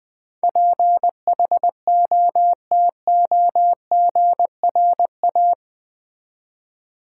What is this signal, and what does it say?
Morse code, heard as a keyed tone, about 21 wpm, 700 Hz. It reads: PHOTOGRA
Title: Build A Word - 4 Letters At A Time - 10-Letter Words - 20wpm